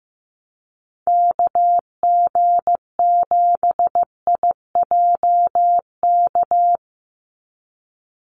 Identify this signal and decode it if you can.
Morse code, heard as a keyed tone, about 15 wpm, 700 Hz. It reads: KG7IJK
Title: USA Call Signs 15wpm